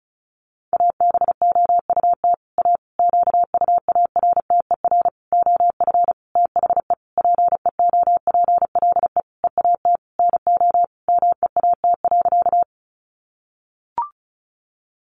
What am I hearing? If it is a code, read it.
Morse code, heard as a keyed tone, about 34 wpm, 700 Hz. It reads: ABOUT A QUARTER OF THE PEOPLE EAT NO MEAT.
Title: Sentences from Top 1300 Words 35wpm